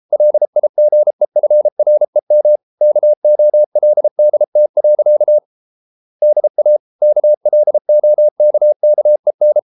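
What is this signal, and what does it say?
Morse code, heard as a keyed tone, about 33 wpm, 600 Hz. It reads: LIGEFREM KOLDT. DA KLOKKEN
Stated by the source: Morse code at 33 wpm, QRQ.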